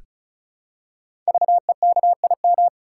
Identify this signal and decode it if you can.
Morse code, heard as a keyed tone, about 36 wpm, 690 Hz. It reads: VEKIM
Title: Code Groups - 5 Letters - 35wpm